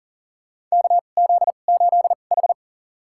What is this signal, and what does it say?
Morse code, heard as a keyed tone, about 40 wpm, spaced as for 29 wpm, 700 Hz. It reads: KZ8H
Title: USA Call Signs - 2x Character Spacing - 40wpm